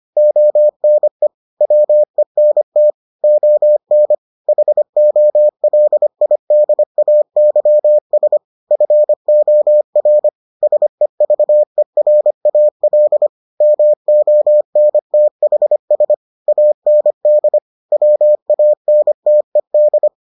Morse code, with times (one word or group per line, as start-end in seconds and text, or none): 0.2-1.3 ONE
1.6-2.9 WENT
3.2-4.1 ON
4.5-8.4 HOLIDAYS
8.7-10.3 FOR
10.6-13.3 SEVERAL
13.6-16.1 MONTHS
16.5-17.6 AND
17.9-20.1 WANTED